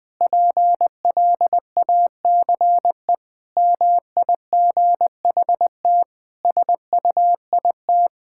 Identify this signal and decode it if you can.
Morse code, heard as a keyed tone, about 20 wpm, 700 Hz. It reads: PLACE MIGHT SUIT
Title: Sentences from Top 1600 Words 20wpm